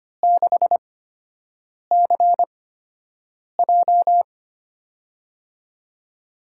Morse code, heard as a keyed tone, about 25 wpm, spaced as for 7 wpm, 700 Hz. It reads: 6CJ
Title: USA Call Signs - 5 Characters - 8x Character Spacing - 25wpm